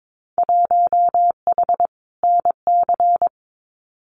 Morse code, heard as a keyed tone, about 22 wpm, 700 Hz. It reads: 1H NC
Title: Winter Field Day Exchanges 22wpm